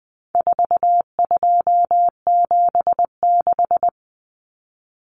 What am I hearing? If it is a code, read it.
Morse code, heard as a keyed tone, about 20 wpm, 700 Hz. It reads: 4276
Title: Common Cut Numbers - 4-Digit 20wpm